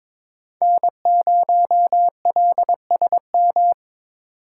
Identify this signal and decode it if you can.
Morse code, heard as a keyed tone, about 22 wpm, 690 Hz. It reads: N0LSM
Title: USA Call Signs - 5 Characters - 2x Character Spacing - 22wpm